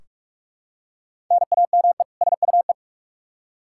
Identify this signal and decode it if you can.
Morse code, heard as a keyed tone, about 44 wpm, 700 Hz. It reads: NAME SUE
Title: Practice QSO - Exchange 1 - Beginning - 45wpm